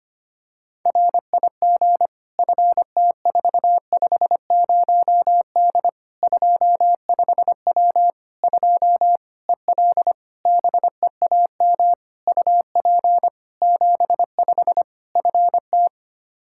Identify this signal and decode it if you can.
Morse code, heard as a keyed tone, about 25 wpm, 700 Hz. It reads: RIG FT450D 25W 2 EL BEAM UP 75 FT
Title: Practice QSO - Exchange 2 - Info - 25wpm